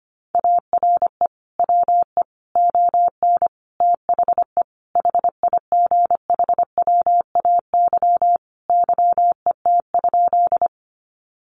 Morse code, heard as a keyed tone, about 25 wpm, 700 Hz. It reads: ARE WE ON THE HIGHWAY YET?
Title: Sentences from Top 1400 Words 25wpm